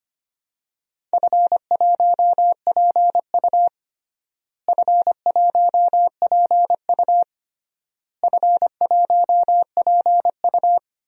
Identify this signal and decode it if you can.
Morse code, heard as a keyed tone, about 25 wpm, 700 Hz. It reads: F1PU F1PU F1PU
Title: DXCC-100 Call Signs - 4 Characters - Repeated 3x - 25wpm